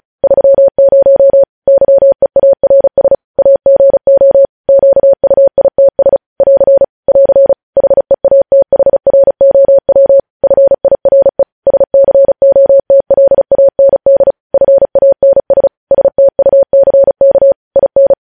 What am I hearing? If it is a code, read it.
Morse code, heard as a keyed tone, about 36 wpm, 550 Hz. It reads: 30 YEARS AGO QUITS <AR> <AR> HEATHROW FIRE SCOTLAND FANS STUCK IN
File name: podcast / News headlines-1x-35x35